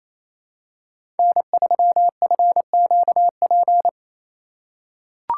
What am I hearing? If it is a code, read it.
Morse code, heard as a keyed tone, about 28 wpm, 700 Hz. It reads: N3FQP
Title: USA Call Signs - 5 Characters - 2x Character Spacing - 28wpm